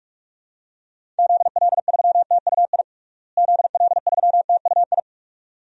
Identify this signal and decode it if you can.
Morse code, heard as a keyed tone, about 47 wpm, 700 Hz. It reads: ZL3TUI ZL3TUI
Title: DXCC-100 Call Signs - 6 Characters - Repeated 2x - 45wpm